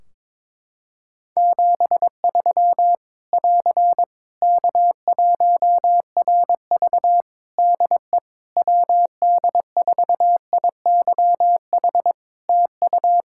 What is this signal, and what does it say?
Morse code, heard as a keyed tone, about 22 wpm, 700 Hz. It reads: 73 <AR> K1RV DE WD4IYH TU
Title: Practice QSO - Exchange 4 - Ending - 22wpm